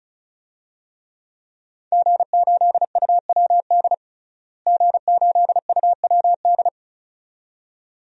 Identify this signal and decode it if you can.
Morse code, heard as a keyed tone, about 36 wpm, 700 Hz. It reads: G8UWD G8UWD
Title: DXCC-100 Call Signs - 5 Characters - Repeated 2x - 35wpm